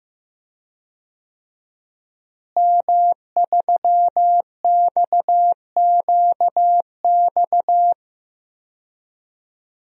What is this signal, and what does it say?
Morse code, heard as a keyed tone, about 15 wpm, 700 Hz. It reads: M3XQX
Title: DXCC-100 Call Signs - 5 Characters - Repeated 3x - 15wpm